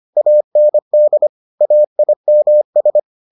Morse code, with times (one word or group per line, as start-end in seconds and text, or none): 0.2-1.3 AND
1.6-3.0 AIMS